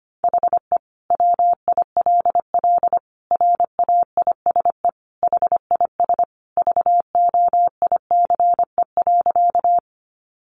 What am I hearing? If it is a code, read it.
Morse code, heard as a keyed tone, about 25 wpm, 700 Hz. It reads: HE WILL RAISE HIS VOICE.
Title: Sentences from Top 1400 Words 25wpm